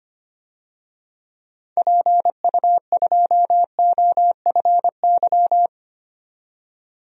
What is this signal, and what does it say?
Morse code, heard as a keyed tone, about 25 wpm, 700 Hz. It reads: PU2OFY
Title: DXCC-100 Call Signs - 6 Characters - 25wpm